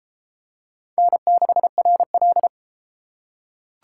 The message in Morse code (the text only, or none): N6RL